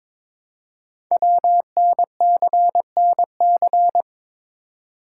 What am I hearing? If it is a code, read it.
Morse code, heard as a keyed tone, about 22 wpm, 700 Hz. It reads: WNCNC
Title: Code Groups - 5 Letters - 22wpm